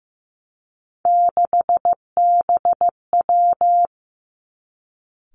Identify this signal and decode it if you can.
Morse code, heard as a keyed tone, about 15 wpm, 700 Hz. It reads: 6BW
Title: Three Challenging Letter-Number Combinations 15wpm